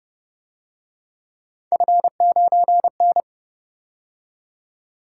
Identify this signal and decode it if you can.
Morse code, heard as a keyed tone, about 30 wpm, 700 Hz. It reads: F9N
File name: Speed-Racing - Three Letter-Number Combinations - 25wpm